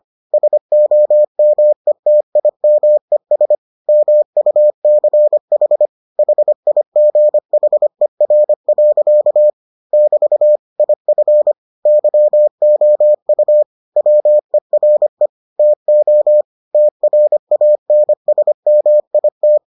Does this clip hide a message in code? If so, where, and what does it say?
Yes: Morse code from 0.3 to 19.6 s, heard as a keyed tone, about 25 wpm, 600 Hz. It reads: SOMETIMES MUCH HIGHER. <BT> IF YOU WERE TO TRANSMIT